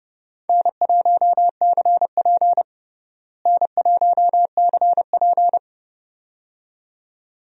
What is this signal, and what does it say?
Morse code, heard as a keyed tone, about 29 wpm, 700 Hz. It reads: N1CP N1CP
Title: USA Call Signs - 4 Characters - Repeated 2x - 30wpm